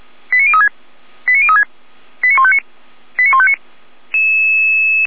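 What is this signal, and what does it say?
Five-tone selective call (ZVEI), heard as quick runs of five tones, tones of about 70 ms. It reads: 89036 89036 89159 89159
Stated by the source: Five-tone ZVEI selective call received on an SDR.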